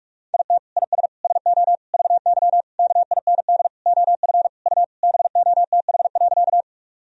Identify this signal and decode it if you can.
Morse code, heard as a keyed tone, about 42 wpm, 700 Hz. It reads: IT IS SO VY KIND OF U BOTH.
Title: Sentences from Top 300 Words and Abbreviations 45wpm